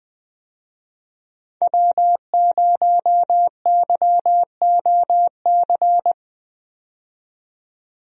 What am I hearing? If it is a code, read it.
Morse code, heard as a keyed tone, about 21 wpm, 700 Hz. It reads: W0YOC
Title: USA Call Signs - 5 Characters - 20wpm